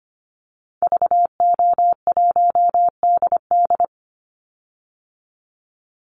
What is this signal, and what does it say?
Morse code, heard as a keyed tone, about 25 wpm, 700 Hz. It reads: VO1DD
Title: Call Signs Repeated 2x - 25wpm